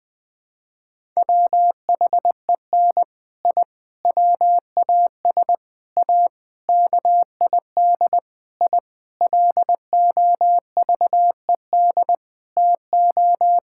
Morse code, heard as a keyed tone, about 20 wpm, 700 Hz. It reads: WHEN I WAS A KID I LOVED TO
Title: Sentences with 6-Word Phrases - Spoken Beginning - 20wpm